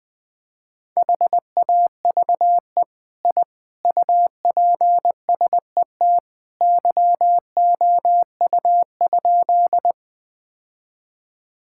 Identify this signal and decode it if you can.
Morse code, heard as a keyed tone, about 20 wpm, 700 Hz. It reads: HAVE I UPSET YOU?
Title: Sentences from Top 1000 Words - Review Part 5 - 20wpm